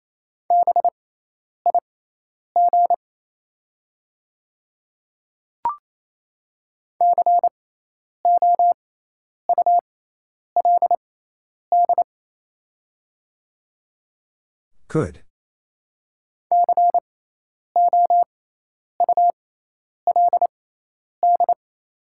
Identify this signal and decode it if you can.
Morse code, heard as a keyed tone, about 28 wpm, spaced as for 10 wpm, 700 Hz. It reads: BIG COULD COULD
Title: 6x Character Spacing - Top 200 Words - 28wpm